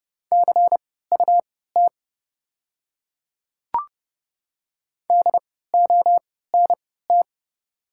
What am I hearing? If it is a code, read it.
Morse code, heard as a keyed tone, about 30 wpm, spaced as for 17 wpm, 690 Hz. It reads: CUT DONT
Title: 3x Character Spacing - Top 300 Words - 30wpm